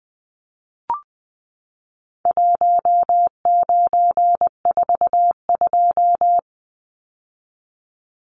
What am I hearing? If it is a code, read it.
Morse code, heard as a keyed tone, about 20 wpm, 700 Hz. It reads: 1942
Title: Numbers - Contest Style - 4-Digit 20wpm